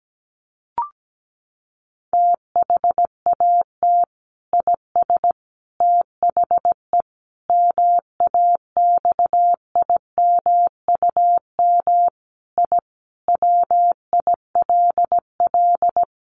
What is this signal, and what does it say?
Morse code, heard as a keyed tone, about 17 wpm, 700 Hz. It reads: THAT IS THE MAXIMUM I WILL
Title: Sentences from Top 1300 Words 17wpm